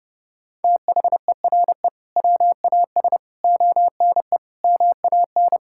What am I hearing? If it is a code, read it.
Morse code, heard as a keyed tone, about 30 wpm, 700 Hz. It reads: THERE WAS ONE MAN
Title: Sentences from Top 1600 Words 30wpm